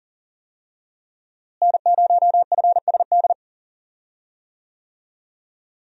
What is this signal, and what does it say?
Morse code, heard as a keyed tone, about 40 wpm, 700 Hz. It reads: N0FSD
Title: USA Call Signs - 5 Characters - Repeated 3x - 40wpm